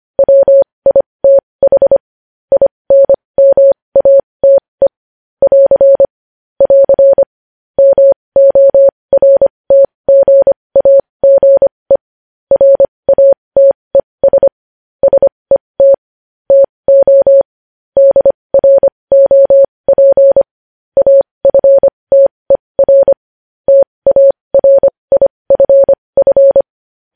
Morse code, heard as a keyed tone, about 25 wpm, spaced as for 20 wpm, 550 Hz. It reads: WITH INMATE <AR> <AR> MORTGAGE RATES SET TO DROP AFTER TARIFF
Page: news headlines in morse code at 20 wpm